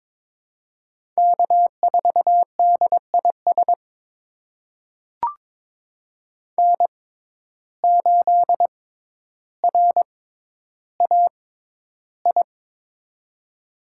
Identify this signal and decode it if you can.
Morse code, heard as a keyed tone, about 21 wpm, spaced as for 11 wpm, 700 Hz. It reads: K4DIS N8RAI
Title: USA Call Signs - 5 Characters - 6x Character Spacing - 22wpm